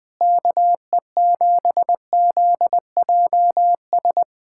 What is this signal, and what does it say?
Morse code, heard as a keyed tone, about 21 wpm, 700 Hz. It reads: KE7ZJS
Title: USA Callsign DE Callsign - 6 Characters - 20wpm